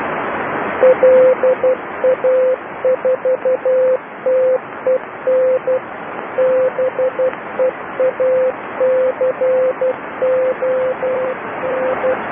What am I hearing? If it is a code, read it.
Morse code, heard as a keyed tone, about 12 wpm, 520 Hz, with noise. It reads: LA4TEN BEACON
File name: LA4TEN beacon 1008 28210 Doucy sito2